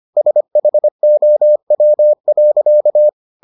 Morse code, heard as a keyed tone, about 25 wpm, 600 Hz. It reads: SHOW.